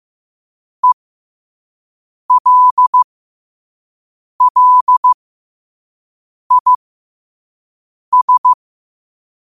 Morse code, heard as a keyed tone, about 15 wpm, spaced as for 5 wpm, 1000 Hz. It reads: ELLIS
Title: 5 WPM morse code quotes for Fri, 01 Mar 2024